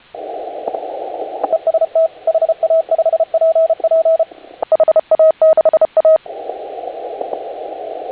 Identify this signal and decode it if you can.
Morse code, heard as a keyed tone, about 31 wpm, 620 Hz, with radio noise. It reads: EST HA5PP HA6A
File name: HA6A CW